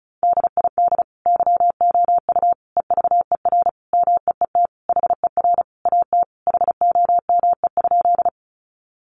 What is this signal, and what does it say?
Morse code, heard as a keyed tone, about 34 wpm, 700 Hz. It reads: DID YOU EVER MEET HER AT HOME?
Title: Sentences from Top 1100 Words 35wpm